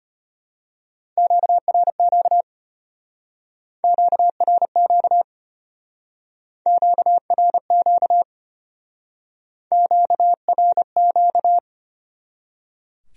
Morse code, heard as a keyed tone, about 32 wpm, 700 Hz. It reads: QRQ QRQ QRQ QRQ